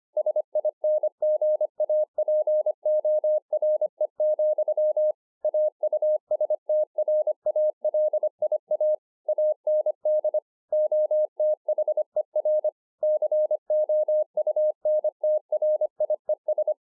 Morse code, heard as a keyed tone, about 26 wpm, 610 Hz. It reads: SINGAPORE, AUSTRALIA AND OTHER COUNTRIES
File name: 2011-04-mp3 / BEGINNERS DIARY 25WPM